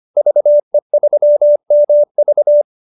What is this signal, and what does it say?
Morse code, heard as a keyed tone, about 25 wpm, 590 Hz. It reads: VE3MV